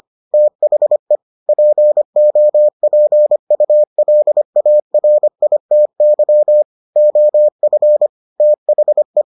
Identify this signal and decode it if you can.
Morse code, heard as a keyed tone, about 25 wpm, 600 Hz. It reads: THE POPULARITY OF THE